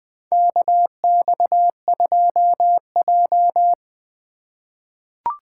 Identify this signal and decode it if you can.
Morse code, heard as a keyed tone, about 20 wpm, 700 Hz. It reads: KX2J
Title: USA Call Signs - 4 Characters - 3x Character Spacing - 20wpm